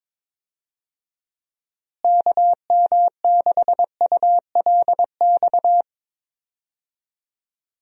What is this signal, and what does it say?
Morse code, heard as a keyed tone, about 22 wpm, 700 Hz. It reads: KM6ULX
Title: USA Call Signs - 6 Characters - Repeated 2x - 22wpm